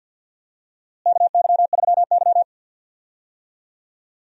Morse code, heard as a keyed tone, about 48 wpm, 700 Hz. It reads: KY3Y
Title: USA Call Signs - 4 Characters - Repeated 2x - 50wpm